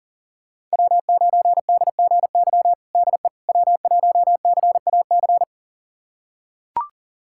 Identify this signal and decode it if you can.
Morse code, heard as a keyed tone, about 40 wpm, 700 Hz. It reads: W9DGY DE W1CAC
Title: USA Callsign DE Callsign - 5 Characters - 40wpm